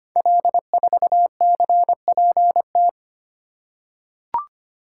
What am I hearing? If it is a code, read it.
Morse code, heard as a keyed tone, about 25 wpm, 700 Hz. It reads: L4CPT
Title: Code Groups - 5 Letter:Numbers - 25wpm